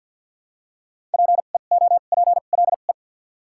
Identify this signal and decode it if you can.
Morse code, heard as a keyed tone, about 53 wpm, spaced as for 37 wpm, 690 Hz. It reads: PEOPLE
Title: 2x Character Spacing - Top 100 Words - 50wpm